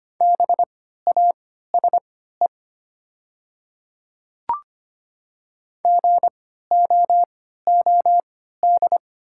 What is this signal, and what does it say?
Morse code, heard as a keyed tone, about 25 wpm, spaced as for 14 wpm, 700 Hz. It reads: BASE GOOD
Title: 3x Character Spacing - Top 300 Words - 25wpm